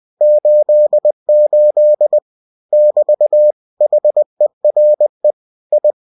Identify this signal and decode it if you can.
Morse code, heard as a keyed tone, about 20 wpm, 600 Hz. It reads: 88 <BT> HERE I
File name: cw-books / winnie-the-pooh / winnie-20wpm-page-0088